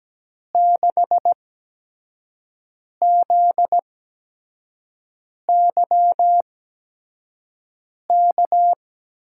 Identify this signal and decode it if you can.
Morse code, heard as a keyed tone, about 17 wpm, spaced as for 5 wpm, 700 Hz. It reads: 6ZYK
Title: USA Call Signs - 6 Characters - 8x Character Spacing - 17wpm